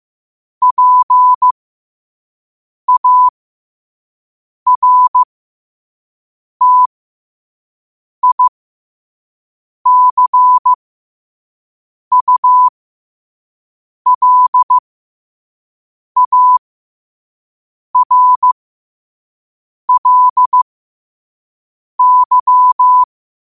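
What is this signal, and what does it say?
Morse code, heard as a keyed tone, about 15 wpm, spaced as for 5 wpm, 1000 Hz. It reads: PARTICULARLY
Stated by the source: Morse code practice at 5 wpm.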